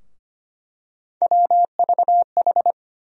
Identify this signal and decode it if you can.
Morse code, heard as a keyed tone, about 25 wpm, 700 Hz. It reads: WVH
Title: USA Call Sign Postfixes - 3 Letters 25wpm